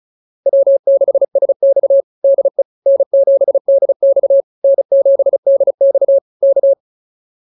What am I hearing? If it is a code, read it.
Morse code, heard as a keyed tone, about 34 wpm, 560 Hz. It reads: W6SX DE N7DX N7DX K
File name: QSO102 35